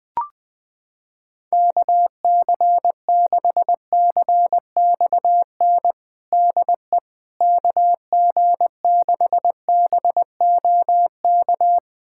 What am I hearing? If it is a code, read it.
Morse code, heard as a keyed tone, about 20 wpm, 700 Hz. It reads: KC6CXN DE KG6BOK
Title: USA Callsign DE Callsign - 6 Characters - 20wpm